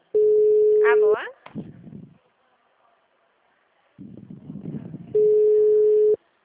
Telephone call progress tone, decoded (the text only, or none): ringback tone